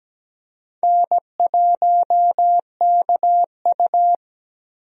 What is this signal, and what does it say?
Morse code, heard as a keyed tone, about 17 wpm, 700 Hz. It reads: N1KU
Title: USA Call Signs - 4 Characters - Repeated 2x - 17wpm